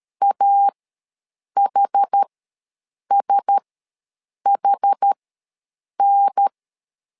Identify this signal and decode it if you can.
Morse code, heard as a keyed tone, about 13 wpm, spaced as for 7 wpm, 800 Hz. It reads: AHSHN